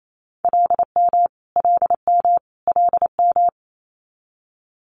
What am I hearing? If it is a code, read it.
Morse code, heard as a keyed tone, about 28 wpm, 700 Hz. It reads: LM LM LM
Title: Multi-letter Phonemes Repeated 3x 28wpm